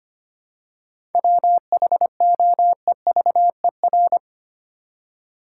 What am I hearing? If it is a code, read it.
Morse code, heard as a keyed tone, about 25 wpm, 700 Hz. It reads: WHOEVER
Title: Seven Letter Words - 25wpm